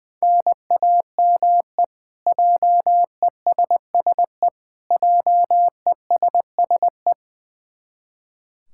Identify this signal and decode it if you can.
Morse code, heard as a keyed tone, about 20 wpm, 700 Hz. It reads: NAME JESSE JESSE
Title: K1USN SST QSO Practice 20wpm